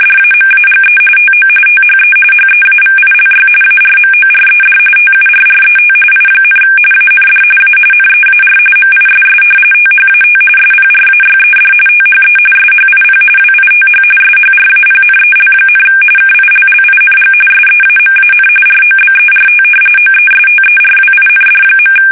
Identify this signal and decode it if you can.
RTTY, about 45.45 baud. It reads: DAMAGE TO THE HEAVILY ARMORED TURRET I, AND A NEAR MISS THAT SPRAYED THE SHIP WITH SHRAPNEL, KILLING SEAMAN ROBERT OESTERWIND.
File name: fsk850 / battleshipnj